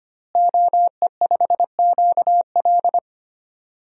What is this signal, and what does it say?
Morse code, heard as a keyed tone, about 25 wpm, 700 Hz. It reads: OE5QL
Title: DXCC-100 Call Signs - 5 Characters - Repeated 2x - 25wpm